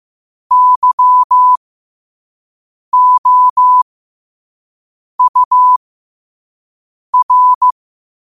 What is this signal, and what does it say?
Morse code, heard as a keyed tone, about 15 wpm, spaced as for 5 wpm, 1000 Hz. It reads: YOUR
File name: qotdfiles / 5wpm